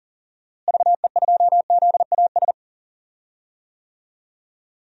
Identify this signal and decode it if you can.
Morse code, heard as a keyed tone, about 40 wpm, 700 Hz. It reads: VE2ZAS
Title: DXCC-100 Call Signs - 6 Characters - Repeated 2x - 40wpm